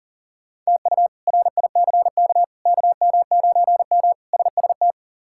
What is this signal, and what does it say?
Morse code, heard as a keyed tone, about 40 wpm, 700 Hz. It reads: TU RICK KM9M SST
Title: K1USN SST QSO Practice - 40wpm